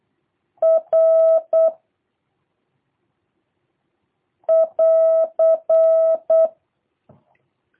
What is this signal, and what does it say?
Morse code, heard as a keyed tone, about 8 wpm, 650 Hz. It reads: R <AR>
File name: jh8cbh / 2023cw / review1-2